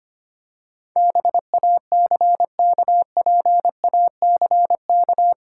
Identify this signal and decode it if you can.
Morse code, heard as a keyed tone, about 26 wpm, 700 Hz. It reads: BACKPACK
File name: practice / 25wpm / Difficult Words - Speed-Racing with Hints - 25wpm